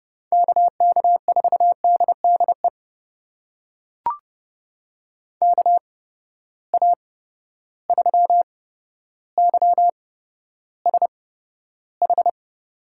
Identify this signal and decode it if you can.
Morse code, heard as a keyed tone, about 30 wpm, spaced as for 13 wpm, 700 Hz. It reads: KK4DDE KA3YSH
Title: USA Call Signs - 6 Characters - 8x Character Spacing - 30wpm